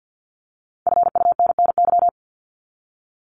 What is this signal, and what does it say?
Morse code, heard as a keyed tone, about 50 wpm, 700 Hz. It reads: FUNNY
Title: Top 1751-2000 Words 50wpm